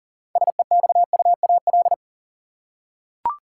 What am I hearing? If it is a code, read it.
Morse code, heard as a keyed tone, about 40 wpm, 700 Hz. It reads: SEXUAL
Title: Six Letter Words - 40wpm